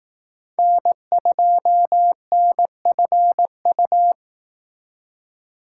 Morse code, heard as a keyed tone, about 18 wpm, 700 Hz. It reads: N2NFU
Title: Speed Racing - USA Call Signs 15wpm